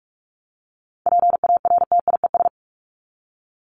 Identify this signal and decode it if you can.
Morse code, heard as a keyed tone, about 44 wpm, 700 Hz. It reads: PARTIES
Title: ES:S Suffix - Top 100 Words with Base Words 45wpm